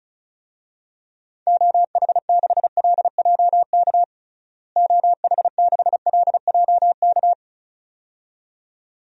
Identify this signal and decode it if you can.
Morse code, heard as a keyed tone, about 34 wpm, 700 Hz. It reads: OH6LJK OH6LJK
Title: DXCC-100 Call Signs - 6 Characters - Repeated 2x - 35wpm